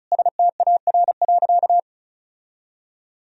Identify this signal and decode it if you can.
Morse code, heard as a keyed tone, about 34 wpm, 700 Hz. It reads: STAR.
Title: Sentences from Top 1000 Words - Review Part 2 - 35wpm